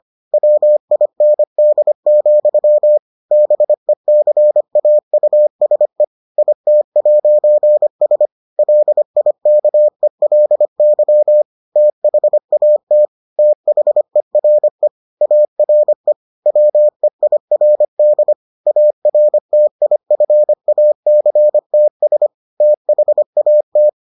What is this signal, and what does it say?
Morse code, heard as a keyed tone, about 25 wpm, 600 Hz. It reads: WIND, BECAUSE IT'S LIKELY THAT THERE ARE WEIRD ARTIFACTS THAT